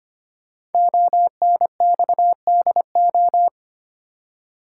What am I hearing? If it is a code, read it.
Morse code, heard as a keyed tone, about 25 wpm, 700 Hz. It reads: ONXDO
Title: Code Groups - 5 Letters - 25wpm